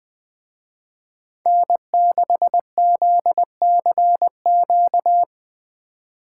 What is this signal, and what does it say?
Morse code, heard as a keyed tone, about 21 wpm, 700 Hz. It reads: N6ZCQ
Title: Speed Racing - USA Call Signs 17wpm